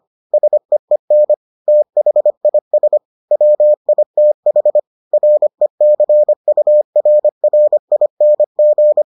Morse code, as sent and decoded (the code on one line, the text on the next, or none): ... . -. / - .... .. ... / .-- .. - .... / .-. -.-. ..- .-. .-. .. -. --.
SEEN THIS WITH RECURRING